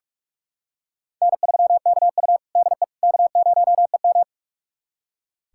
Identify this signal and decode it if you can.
Morse code, heard as a keyed tone, about 46 wpm, 700 Hz. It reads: N3KU DE K0EM
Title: USA Callsign DE Callsign - 4 Characters - 45wpm